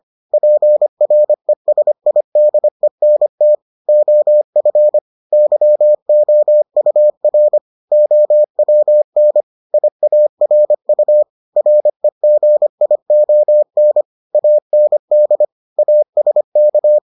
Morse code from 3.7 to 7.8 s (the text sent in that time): OF YOUR